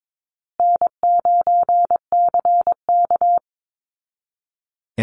Morse code, heard as a keyed tone, about 22 wpm, 700 Hz. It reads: N9CK
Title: Easy Call Signs Repeated 3x - 22wpm